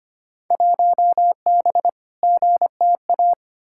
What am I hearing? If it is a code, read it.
Morse code, heard as a keyed tone, about 25 wpm, 700 Hz. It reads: 1B GTA
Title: Field Day Exchanges Encore - 25wpm